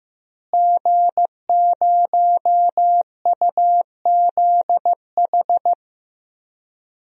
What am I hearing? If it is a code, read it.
Morse code, heard as a keyed tone, about 15 wpm, 700 Hz. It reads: G0UZH
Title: DXCC-100 Call Signs - 5 Characters - Repeated 2x - 15wpm